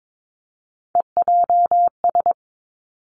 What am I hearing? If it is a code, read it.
Morse code, heard as a keyed tone, about 22 wpm, 700 Hz. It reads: EJS
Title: Three Challenging Letter-Number Combinations 22wpm